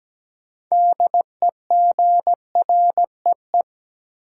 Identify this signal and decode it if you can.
Morse code, heard as a keyed tone, about 17 wpm, 700 Hz. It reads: DEGREE
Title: Sentences from Top 1500 Words - New Word Drill Alt - 17wpm